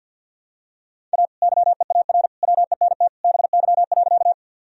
Morse code, heard as a keyed tone, about 46 wpm, 700 Hz. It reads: A YEAR WENT BY.
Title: Sentences from Top 1000 Words - Review Part 1 - 50wpm